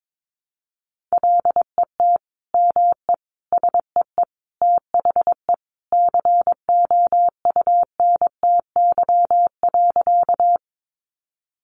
Morse code, heard as a keyed tone, about 22 wpm, 700 Hz. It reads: LET ME SEE THE COUNTY.
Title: Sentences from Top 1200 Words 22wpm